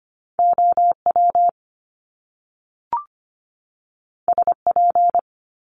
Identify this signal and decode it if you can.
Morse code, heard as a keyed tone, about 25 wpm, 700 Hz. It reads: OW SP
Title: Top Two Letter Combinations in Words 25wpm